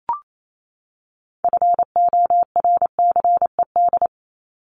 Morse code, heard as a keyed tone, about 28 wpm, 700 Hz. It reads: FORCED
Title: ED Suffix - Top 100 Words with Base Words 28wpm